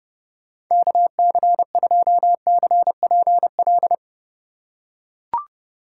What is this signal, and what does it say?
Morse code, heard as a keyed tone, about 30 wpm, 700 Hz. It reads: KC2CPL
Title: USA Call Signs - 6 Characters - 2x Character Spacing - 30wpm